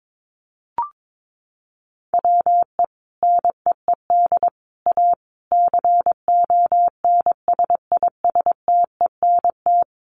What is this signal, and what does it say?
Morse code, heard as a keyed tone, about 22 wpm, 700 Hz. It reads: WE NEED A CONSISTENT
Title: Sentences from Top 1500 Words 22wpm